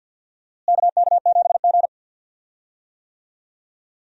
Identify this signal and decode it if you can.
Morse code, heard as a keyed tone, about 47 wpm, 700 Hz. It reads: KK7G
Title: USA Call Signs - 4 Characters - Repeated 3x - 50wpm